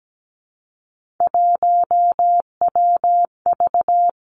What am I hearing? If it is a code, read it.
Morse code, heard as a keyed tone, about 17 wpm, 700 Hz. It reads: 1WV
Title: Three Challenging Letter-Number Combinations 17wpm